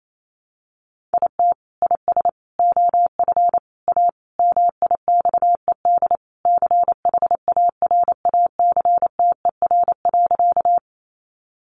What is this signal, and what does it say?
Morse code, heard as a keyed tone, about 28 wpm, 700 Hz. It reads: IT IS OF A MIXED CHARACTER.
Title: Sentences from Top 1500 Words 28wpm